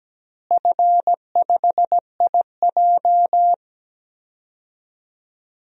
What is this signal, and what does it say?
Morse code, heard as a keyed tone, about 17 wpm, 700 Hz. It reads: F5IJ
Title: DXCC-100 Call Signs - 4 Characters - Repeated 2x - 17wpm